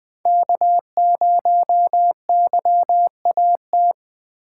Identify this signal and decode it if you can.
Morse code, heard as a keyed tone, about 20 wpm, 700 Hz. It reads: K0YAT
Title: USA Call Signs - 5 Characters - 2x Character Spacing - 20wpm